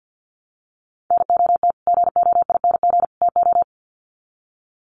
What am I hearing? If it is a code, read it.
Morse code, heard as a keyed tone, about 49 wpm, 700 Hz. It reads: NOT GOING TO
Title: Sets of 3 Words 50wpm